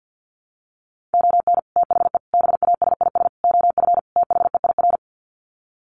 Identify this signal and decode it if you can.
Morse code, heard as a keyed tone, about 50 wpm, 700 Hz. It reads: ON THE BASIS OF THEIR
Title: Sets of 5 Words Encore 50wpm